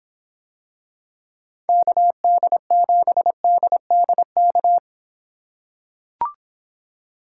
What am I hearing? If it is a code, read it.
Morse code, heard as a keyed tone, about 26 wpm, 700 Hz. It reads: KD7DDK